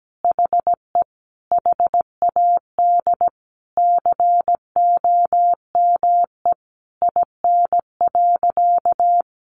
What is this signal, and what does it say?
Morse code, heard as a keyed tone, about 17 wpm, 700 Hz. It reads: HE HAD COME IN.
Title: Sentences from Top 1000 Words 17wpm